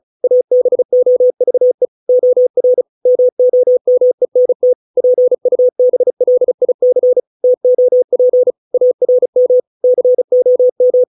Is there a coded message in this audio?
Yes: Morse code from 0.2 to 11.0 s, heard as a keyed tone, about 34 wpm, 500 Hz. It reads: ABOVE OR MOMENT PUBLIC TOP ARM COM